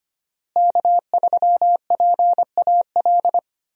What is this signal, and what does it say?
Morse code, heard as a keyed tone, about 25 wpm, 700 Hz. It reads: K3PAL
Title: USA Call Signs - 5 Characters - Repeated 2x - 25wpm